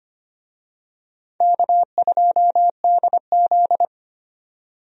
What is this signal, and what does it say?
Morse code, heard as a keyed tone, about 25 wpm, 700 Hz. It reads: K2DZ
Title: USA Call Signs - 4 Characters - 25wpm